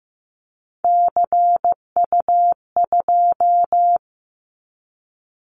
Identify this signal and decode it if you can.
Morse code, heard as a keyed tone, about 15 wpm, 700 Hz. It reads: CU2
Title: Callsign Prefixes for Top 100 DXCC - 15wpm